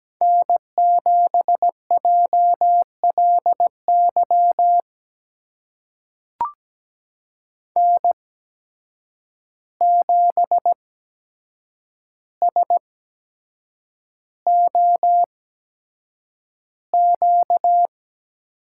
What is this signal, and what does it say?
Morse code, heard as a keyed tone, about 17 wpm, spaced as for 7 wpm, 700 Hz. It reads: N7JLY N7SOQ
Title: USA Call Signs - 5 Characters - 8x Character Spacing - 17wpm